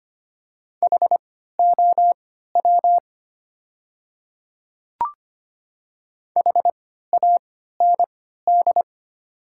Morse code, heard as a keyed tone, about 25 wpm, spaced as for 14 wpm, 700 Hz. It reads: HOW HAND
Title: 3x Character Spacing - Top 200 Words - 25wpm